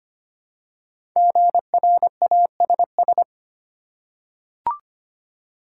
Morse code, heard as a keyed tone, about 25 wpm, 700 Hz. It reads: GRASS
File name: 5 Letter Words - First 3 Letters Given - 25wpm